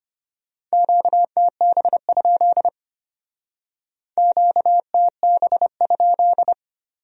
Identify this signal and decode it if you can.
Morse code, heard as a keyed tone, about 33 wpm, 700 Hz. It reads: QTB? QTB?